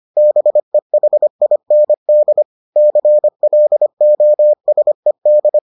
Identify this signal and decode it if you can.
Morse code, heard as a keyed tone, about 25 wpm, 600 Hz. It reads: BEHIND CLOSED